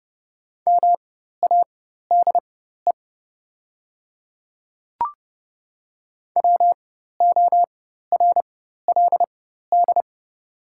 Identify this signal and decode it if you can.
Morse code, heard as a keyed tone, about 31 wpm, spaced as for 14 wpm, 700 Hz. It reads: MADE WORLD
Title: 4x Character Spacing - Top 200 Words - 30wpm